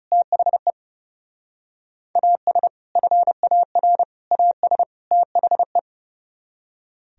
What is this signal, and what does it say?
Morse code, heard as a keyed tone, about 35 wpm, 700 Hz. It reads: THE AS FAR AS THE
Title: Speed-Racing - Sets of 4 Words 30wpm